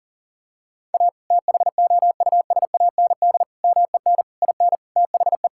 Morse code, heard as a keyed tone, about 37 wpm, 700 Hz. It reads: A THOUSAND MEN IN THE
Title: Sentences from Top 1000 Words - Review Part 6 - 40wpm